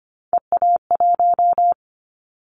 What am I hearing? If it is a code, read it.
Morse code, heard as a keyed tone, about 25 wpm, 700 Hz. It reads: EA1
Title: Callsign Prefixes for Top 25 DXCC - Repeated 3x - 25wpm